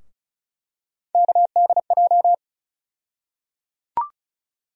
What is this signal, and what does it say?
Morse code, heard as a keyed tone, about 36 wpm, 700 Hz. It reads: KDJ
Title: USA Call Sign Postfixes - 3 Letters 35wpm